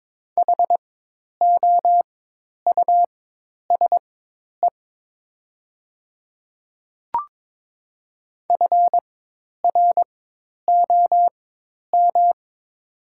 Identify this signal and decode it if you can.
Morse code, heard as a keyed tone, about 22 wpm, spaced as for 10 wpm, 700 Hz. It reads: HOUSE FROM
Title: 4x Character Spacing - Top 200 Words - 22wpm